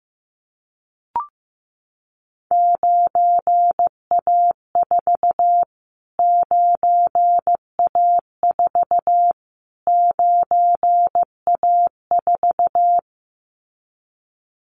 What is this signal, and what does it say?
Morse code, heard as a keyed tone, about 15 wpm, 700 Hz. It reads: 9A4 9A4 9A4
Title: Callsign Prefixes for Top 25 DXCC - Repeated 3x - 15wpm